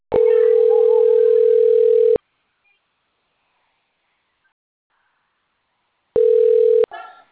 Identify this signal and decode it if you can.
Ringback tone (North America), 440 + 480 Hz.